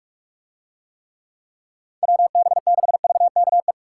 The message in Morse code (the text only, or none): WB6VKE